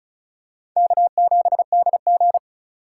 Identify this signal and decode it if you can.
Morse code, heard as a keyed tone, about 36 wpm, 700 Hz. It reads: K7DG
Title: USA Call Signs - 4 Characters - 4x Character Spacing - 35wpm